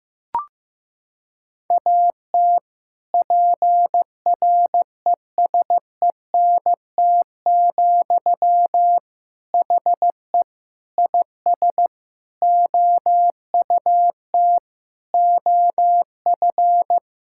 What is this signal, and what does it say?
Morse code, heard as a keyed tone, about 16 wpm, 700 Hz. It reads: AT PRESENT, HE IS OUT OF
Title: Sentences from Top 1000 Words - Review Part 5 - 15wpm